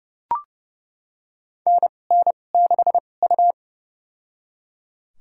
Morse code, heard as a keyed tone, about 30 wpm, spaced as for 22 wpm, 700 Hz. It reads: NN6U
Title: USA Call Signs - 2x Character Spacing - 30wpm